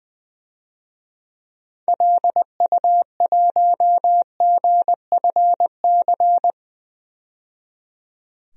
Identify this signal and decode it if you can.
Morse code, heard as a keyed tone, about 20 wpm, 700 Hz. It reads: LU1GFC
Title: DXCC-100 Call Signs - 6 Characters - 20wpm